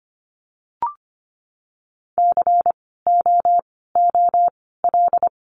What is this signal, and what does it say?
Morse code, heard as a keyed tone, about 25 wpm, 700 Hz. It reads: C O O L
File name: course / 25:10wpm / 094 Lesson 16f - Letter c - Words 25wpm